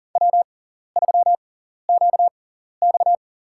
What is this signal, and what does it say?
Morse code, heard as a keyed tone, about 40 wpm, spaced as for 14 wpm, 700 Hz. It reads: W3QX
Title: USA Call Signs - 4 Characters - 6x Character Spacing - 40wpm